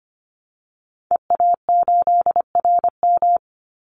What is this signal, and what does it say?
Morse code, heard as a keyed tone, about 25 wpm, 700 Hz. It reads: EA8RM
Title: Call Signs - 25wpm